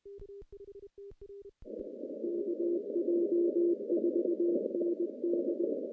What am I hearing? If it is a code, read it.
Morse code, heard as a keyed tone, about 24 wpm, 340 Hz. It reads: KJ4ND
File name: contests / CW-SS-k5tr- -0000